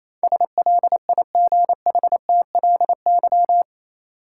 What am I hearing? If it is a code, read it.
Morse code, heard as a keyed tone, about 28 wpm, 700 Hz. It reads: SLIGHTLY
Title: Sentences from Top 1300 Words - New Word Drill - 28wpm